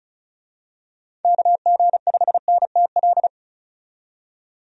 Morse code, heard as a keyed tone, about 36 wpm, 700 Hz. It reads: KG5NTL
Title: USA Call Signs - 6 Characters - 35wpm